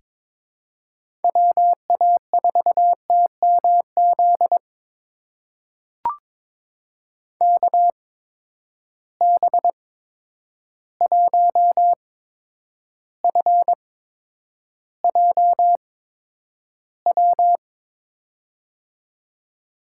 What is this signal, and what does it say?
Morse code, heard as a keyed tone, about 22 wpm, spaced as for 9 wpm, 700 Hz. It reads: WA4TMZ KB1FJW
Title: USA Call Signs - 6 Characters - 8x Character Spacing - 22wpm